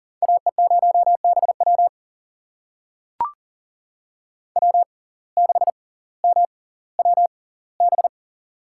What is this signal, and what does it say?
Morse code, heard as a keyed tone, about 39 wpm, spaced as for 20 wpm, 690 Hz. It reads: AE0BW W6MWB
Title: USA Call Signs - 5 Characters - 6x Character Spacing - 40wpm